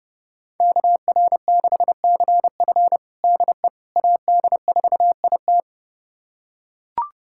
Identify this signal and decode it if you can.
Morse code, heard as a keyed tone, about 32 wpm, 700 Hz. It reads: KR6CF DE AD4IT
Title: USA Callsign DE Callsign - 5 Characters - 30wpm